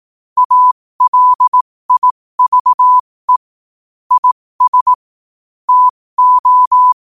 Morse code, heard as a keyed tone, about 18 wpm, spaced as for 15 wpm, 1000 Hz. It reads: ALIVE IS TO
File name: qotdfiles / 15wpm